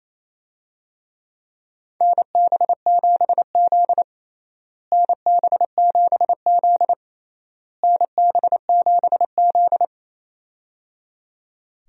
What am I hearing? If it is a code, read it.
Morse code, heard as a keyed tone, about 27 wpm, 700 Hz. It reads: NB7Z NB7Z NB7Z
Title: USA Call Signs - 4 Characters - Repeated 3x - 28wpm